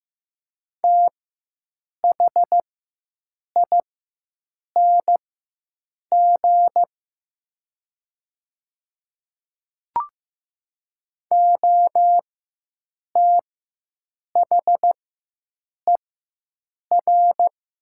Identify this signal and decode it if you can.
Morse code, heard as a keyed tone, about 15 wpm, spaced as for 7 wpm, 700 Hz. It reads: THING OTHER
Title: 4x Character Spacing - Top 200 Words - 15wpm